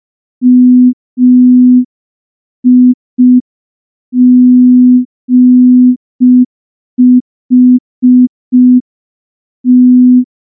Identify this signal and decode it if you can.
Morse code, heard as a keyed tone, about 5 wpm, 250 Hz. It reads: MIGHT
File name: morse / morse-training-05WPM